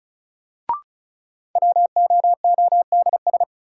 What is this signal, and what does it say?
Morse code, heard as a keyed tone, about 36 wpm, 700 Hz. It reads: WOODS
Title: Sentences from Top 1000 Words - Review Part 2 - New Word Drill - 35wpm